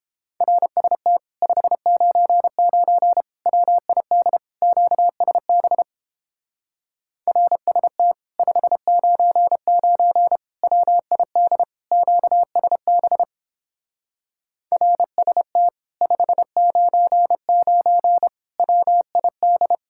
Morse code, heard as a keyed tone, about 31 wpm, 700 Hz. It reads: RST 599 WID QSB RST 599 WID QSB RST 599 WID